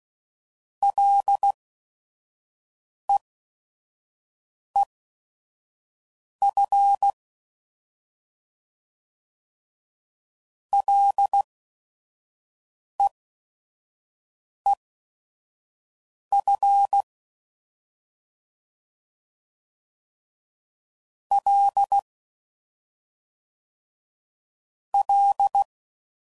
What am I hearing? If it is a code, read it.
Morse code, heard as a keyed tone, about 16 wpm, spaced as for 5 wpm, 800 Hz. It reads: LEEF LEEF L L